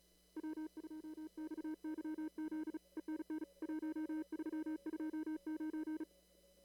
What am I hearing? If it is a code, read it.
Morse code, heard as a keyed tone, about 38 wpm, 310 Hz. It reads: W2XYZ ENN 1329
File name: CW / kt5j wpx cw-op-k5pi